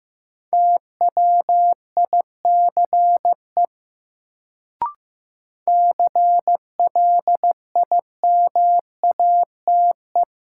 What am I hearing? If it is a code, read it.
Morse code, heard as a keyed tone, about 15 wpm, 700 Hz. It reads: TWICE CLIMATE
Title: Sentences from Top 1600 Words - New Word Drill - 15wpm